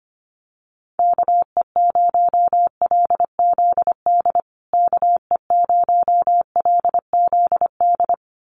Morse code, heard as a keyed tone, about 25 wpm, 700 Hz. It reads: KE0LZD KE0LZD
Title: USA Call Signs Repeated 2x 25wpm